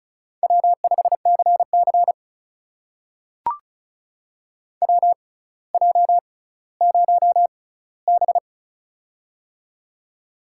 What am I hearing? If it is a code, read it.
Morse code, heard as a keyed tone, about 36 wpm, spaced as for 18 wpm, 700 Hz. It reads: W5CC WJ0B
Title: USA Call Signs - 6x Character Spacing - 35wpm